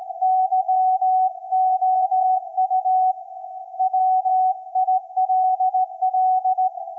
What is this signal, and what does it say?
Morse code, heard as a keyed tone, about 17 wpm, 750 Hz. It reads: YOU WILL
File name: iz7slz / VLF / SAQ 24-12-2024